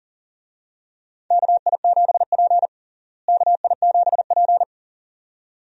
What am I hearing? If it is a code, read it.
Morse code, heard as a keyed tone, about 40 wpm, 700 Hz. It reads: KI7P KI7P
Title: USA Call Signs - 4 Characters - Repeated 2x - 40wpm